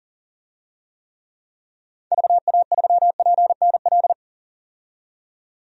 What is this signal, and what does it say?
Morse code, heard as a keyed tone, about 40 wpm, 700 Hz. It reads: VA3PNL
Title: DXCC-100 Call Signs - 6 Characters - 40wpm